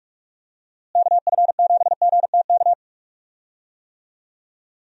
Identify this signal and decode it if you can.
Morse code, heard as a keyed tone, about 46 wpm, 700 Hz. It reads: KF7GTK
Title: USA Call Signs - 6 Characters - Repeated 2x - 45wpm